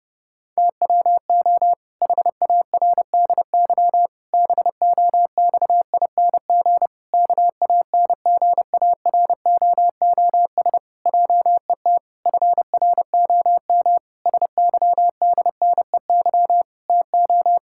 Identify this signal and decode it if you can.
Morse code, heard as a keyed tone, about 30 wpm, 700 Hz. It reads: TWO HARDY BOXING KANGAROOS JET FROM SYDNEY TO
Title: Pangram Challenge v2 - 30wpm